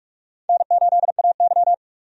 Morse code, heard as a keyed tone, about 46 wpm, 700 Hz. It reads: N8AY
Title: USA Call Signs - 4 Characters - Repeated 3x - 45wpm